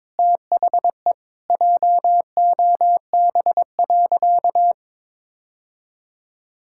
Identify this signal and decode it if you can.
Morse code, heard as a keyed tone, about 21 wpm, 700 Hz. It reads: THE JOB.
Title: Sentences from Top 1600 Words 22wpm